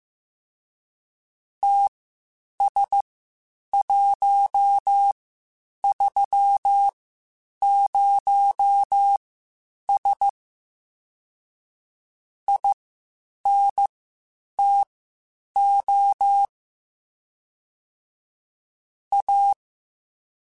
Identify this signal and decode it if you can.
Morse code, heard as a keyed tone, about 15 wpm, spaced as for 8 wpm, 800 Hz. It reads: TS130S INTO A
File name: (spaced) / QSOs Spaced-mp3s / WD0BGP 1 @13 sp